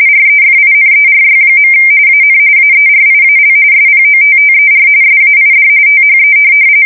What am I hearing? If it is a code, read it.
RTTY, about 45.45 baud. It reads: REFINERY. AN AERIAL SPOTTER SAID THE
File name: fsk170 / USS-New-Jersey